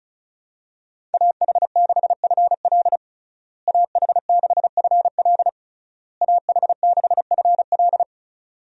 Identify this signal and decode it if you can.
Morse code, heard as a keyed tone, about 34 wpm, 700 Hz. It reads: AH6FL AH6FL AH6FL
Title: USA Call Signs - 5 Characters - Repeated 3x - 35wpm